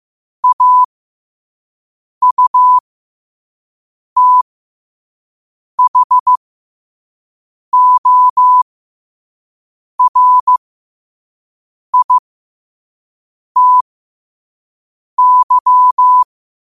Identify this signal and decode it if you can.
Morse code, heard as a keyed tone, about 15 wpm, spaced as for 5 wpm, 1000 Hz. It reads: AUTHORITY
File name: qotdfiles / 5wpm